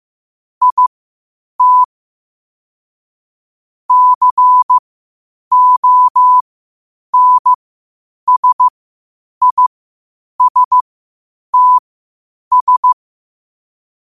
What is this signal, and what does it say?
Morse code, heard as a keyed tone, about 15 wpm, spaced as for 8 wpm, 1000 Hz. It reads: IT CONSISTS
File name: qotdfiles / 8wpm